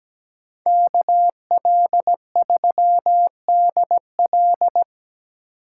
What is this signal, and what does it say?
Morse code, heard as a keyed tone, about 20 wpm, 700 Hz. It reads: KL3DL
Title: Speed Racing - USA Call Signs 17wpm